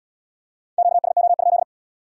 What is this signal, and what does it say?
Morse code, heard as a keyed tone, about 75 wpm, 700 Hz. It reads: QSO?